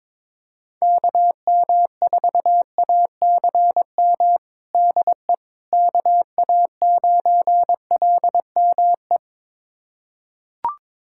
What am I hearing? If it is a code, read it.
Morse code, heard as a keyed tone, about 22 wpm, 700 Hz. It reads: KM4ACM DE KA9LME
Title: USA Callsign DE Callsign 22wpm